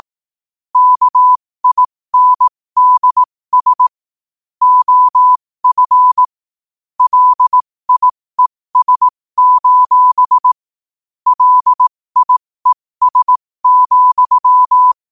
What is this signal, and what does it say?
Morse code, heard as a keyed tone, about 18 wpm, spaced as for 15 wpm, 1000 Hz. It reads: KINDS OF LIES: LIES,
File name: qotdfiles / 15wpm